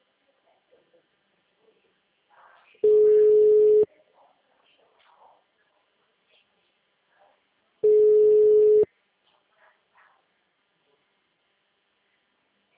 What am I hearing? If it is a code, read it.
Ringback tone (Europe, 425 Hz), with speech.